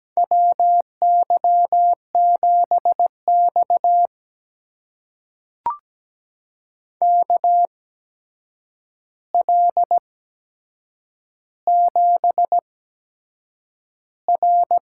Morse code, heard as a keyed tone, about 17 wpm, spaced as for 7 wpm, 700 Hz. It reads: WY7X KL7R
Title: USA Call Signs - 4 Characters - 8x Character Spacing - 17wpm